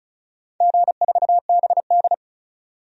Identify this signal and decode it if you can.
Morse code, heard as a keyed tone, about 36 wpm, 700 Hz. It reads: G4BD
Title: DXCC-100 Call Signs - 4 Characters - 35wpm